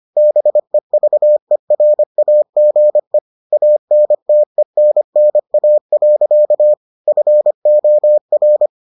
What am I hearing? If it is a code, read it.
Morse code, heard as a keyed tone, about 25 wpm, 600 Hz. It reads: BEVERAGE ANTENNA. FOR